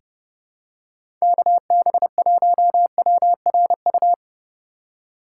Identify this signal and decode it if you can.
Morse code, heard as a keyed tone, about 32 wpm, 700 Hz. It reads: KB1WRU
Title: USA Call Signs - 6 Characters - 4x Character Spacing - 30wpm